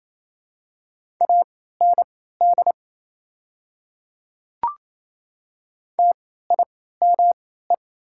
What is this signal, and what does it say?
Morse code, heard as a keyed tone, about 28 wpm, spaced as for 16 wpm, 700 Hz. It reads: AND TIME
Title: 3x Character Spacing - Top 200 Words - 28wpm